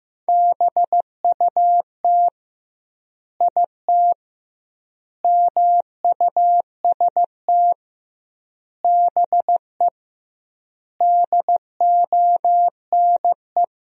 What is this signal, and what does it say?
Morse code, heard as a keyed tone, about 15 wpm, 700 Hz. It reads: BUT IT MUST BE DONE
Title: 2x Word-Spacing - Sentences from Top 500 Words - 15wpm